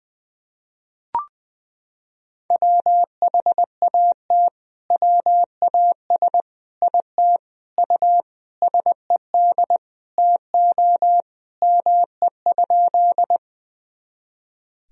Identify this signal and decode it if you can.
Morse code, heard as a keyed tone, about 20 wpm, 700 Hz. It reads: WHAT WAS IT U SED TO ME?
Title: Sentences from Top 200 Words and Abbreviations 20wpm